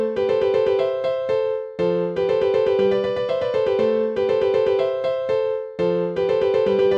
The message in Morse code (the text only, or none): none